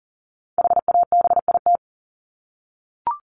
Morse code, heard as a keyed tone, about 40 wpm, 700 Hz. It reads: HABIT
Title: Top 1751-2000 Words 40wpm